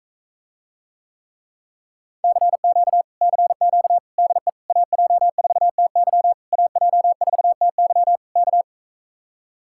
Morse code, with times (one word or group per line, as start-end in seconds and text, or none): 2.2-3.0 CQ
3.2-4.0 CQ
4.2-4.5 DE
4.7-6.3 AJ4TY
6.5-8.2 AJ4TY
8.4-8.6 K